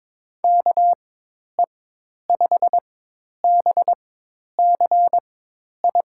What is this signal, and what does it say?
Morse code, heard as a keyed tone, about 22 wpm, spaced as for 10 wpm, 700 Hz. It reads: KE5BCI
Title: USA Call Signs - 6 Characters - 4x Character Spacing - 22wpm